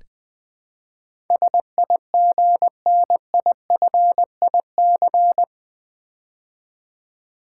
Morse code, heard as a keyed tone, about 20 wpm, 700 Hz. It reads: SIGNIFIC
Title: Build A Word - 4 Letters At A Time - 12-Letter Words - 20wpm